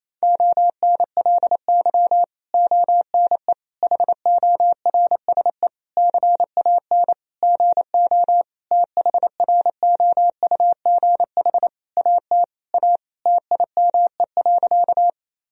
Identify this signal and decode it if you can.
Morse code, heard as a keyed tone, about 28 wpm, 700 Hz. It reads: ONLY ONE HORSE CAN GO THROUGH AT A TIME.
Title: Sentences from Top 1000 Words - Review Part 1 - 28wpm